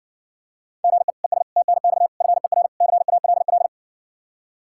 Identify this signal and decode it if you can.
Morse code, heard as a keyed tone, about 64 wpm, 700 Hz. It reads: GE ES TNX FER CALL